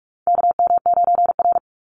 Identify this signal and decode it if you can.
Morse code, heard as a keyed tone, about 44 wpm, 700 Hz. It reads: KM9R
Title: Call Signs - 45wpm